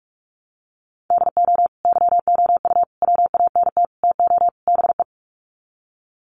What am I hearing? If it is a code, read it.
Morse code, heard as a keyed tone, about 44 wpm, 700 Hz. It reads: DO YOU WANT TO BE
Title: Sets of 5 Words Encore2 45wpm